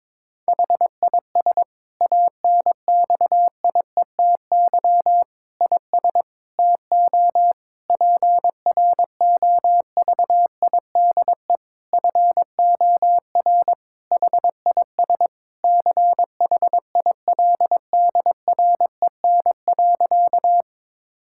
Morse code, heard as a keyed tone, about 22 wpm, 700 Hz. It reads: HIS ANXIETY IS TO PROVIDE FOR HIS CHILDREN.
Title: Sentences from Top 1600 Words 22wpm